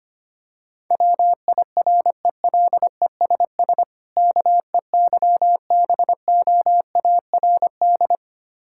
Morse code, heard as a keyed tone, about 25 wpm, 690 Hz. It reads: WIRELESS KEYBOARD
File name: practice / 25wpm / Difficult Words with Hints - 25wpm